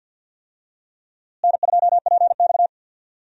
Morse code, heard as a keyed tone, about 53 wpm, 700 Hz. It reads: N2PX
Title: USA Call Signs - 4 Characters - Repeated 3x - 50wpm